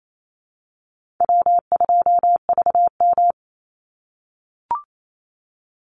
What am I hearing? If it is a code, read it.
Morse code, heard as a keyed tone, about 28 wpm, 700 Hz. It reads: W2VM
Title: Easy Call Signs - 28wpm